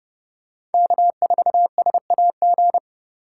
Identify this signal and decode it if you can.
Morse code, heard as a keyed tone, about 30 wpm, 700 Hz. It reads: K4SAG
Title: USA Call Signs - 5 Characters - 3x Character Spacing - 30wpm